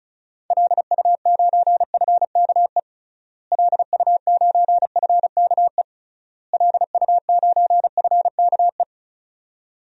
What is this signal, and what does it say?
Morse code, heard as a keyed tone, about 36 wpm, 700 Hz. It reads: LU9FKE LU9FKE LU9FKE
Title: DXCC-100 Call Signs - 6 Characters - Repeated 3x - 35wpm